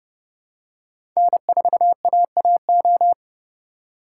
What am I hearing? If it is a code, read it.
Morse code, heard as a keyed tone, about 30 wpm, 700 Hz. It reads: N4AAO
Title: USA Call Signs - 2x Character Spacing - 30wpm